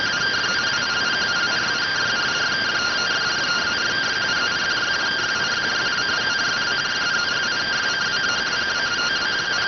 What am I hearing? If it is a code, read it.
Two-tone teleprinter signal, about 170 Hz shift.